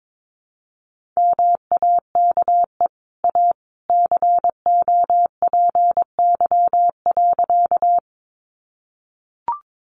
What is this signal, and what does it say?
Morse code, heard as a keyed tone, about 22 wpm, 700 Hz. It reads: MAKE A COPY.
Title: Sentences from Top 1400 Words 22wpm